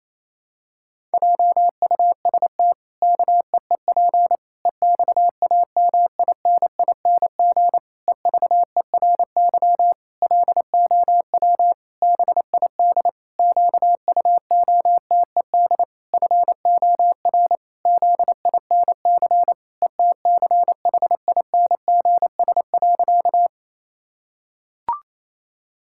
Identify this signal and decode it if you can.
Morse code, heard as a keyed tone, about 28 wpm, 700 Hz. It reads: JUST KEEP EXAMINING EVERY LOW BID QUOTED FOR ZINC ETCHINGS.
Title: Pangram Challenge v2 - 28wpm